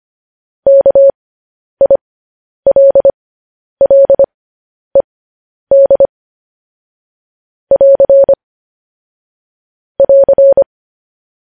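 Morse code, heard as a keyed tone, about 25 wpm, spaced as for 10 wpm, 550 Hz. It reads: KILLED <AR> <AR>
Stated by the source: Morse code practice sending news headlines at 10 wpm.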